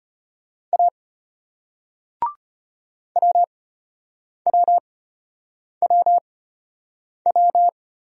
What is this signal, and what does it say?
Morse code, heard as a keyed tone, about 31 wpm, spaced as for 8 wpm, 690 Hz. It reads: A WWWW
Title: Speed-Racing Single Letter-Number 25wpm